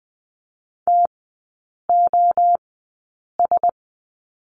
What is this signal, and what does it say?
Morse code, heard as a keyed tone, about 20 wpm, spaced as for 8 wpm, 700 Hz. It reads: TOS
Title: 053 Lesson 10e - Letter r - Three Characters 20wpm